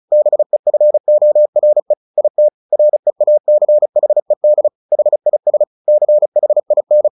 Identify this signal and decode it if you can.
Morse code, heard as a keyed tone, about 34 wpm, 600 Hz. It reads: BEFORE IT REACHED HIS CHIN
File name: the old man and the sea / the old man and the sea 1 35wpm